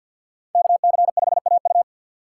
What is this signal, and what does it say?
Morse code, heard as a keyed tone, about 52 wpm, 700 Hz. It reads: KC5AU
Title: USA Call Signs - 5 Characters - 6x Character Spacing - 50wpm